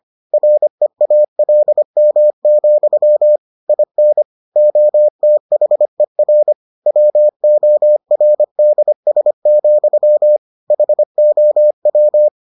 Morse code, heard as a keyed tone, about 25 wpm, 600 Hz. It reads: REALM, IN OTHER WORDS, HOW